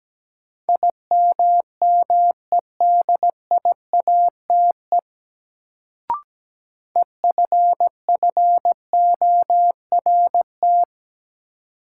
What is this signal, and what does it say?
Morse code, heard as a keyed tone, about 17 wpm, 700 Hz. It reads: IMMEDIATE EFFORT
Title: Sentences from Top 1200 Words - New Word Drill - 17wpm